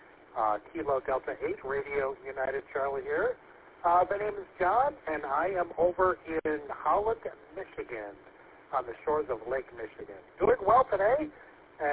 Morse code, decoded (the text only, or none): none